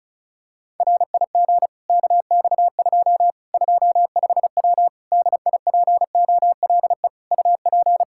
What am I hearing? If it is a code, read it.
Morse code, heard as a keyed tone, about 34 wpm, 700 Hz. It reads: RIG KX2 25W DIPOLE UP
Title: Practice QSO - Exchange 2 - Info - 35wpm